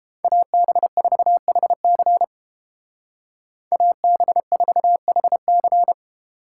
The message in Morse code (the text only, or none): AB4HC AB4HC